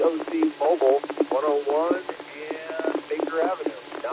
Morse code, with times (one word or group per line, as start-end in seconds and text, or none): none